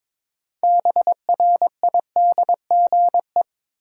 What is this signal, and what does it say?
Morse code, heard as a keyed tone, about 21 wpm, 700 Hz. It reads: BRIDGE
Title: Sentences from Top 1200 Words - New Word Drill Alt - 22wpm